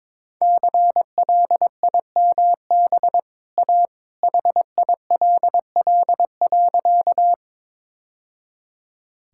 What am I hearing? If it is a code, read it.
Morse code, heard as a keyed tone, about 22 wpm, 700 Hz. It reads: CLIMB A HILL.
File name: practice / 22wpm / Easy Sentences with Hints - 22wpm